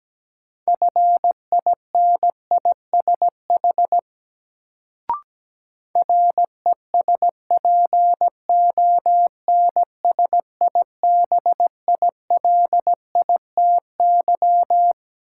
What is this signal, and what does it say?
Morse code, heard as a keyed tone, about 17 wpm, 700 Hz. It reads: FINISH RESPONSIBILITY
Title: Sentences from Top 1200 Words - New Word Drill Alt - 17wpm